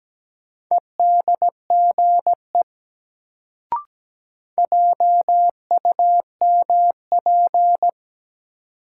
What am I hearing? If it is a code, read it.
Morse code, heard as a keyed tone, about 17 wpm, 700 Hz. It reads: EDGE JUMP
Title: Sentences from Top 1300 Words - New Word Drill - 17wpm